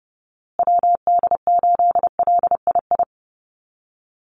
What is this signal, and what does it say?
Morse code, heard as a keyed tone, about 30 wpm, 700 Hz. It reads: WD8LII
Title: USA Call Signs Repeated 2x 30wpm